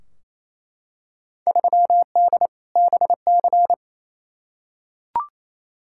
Morse code, heard as a keyed tone, about 28 wpm, 700 Hz. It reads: 3D BC
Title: Field Day Exchanges Encore - 28wpm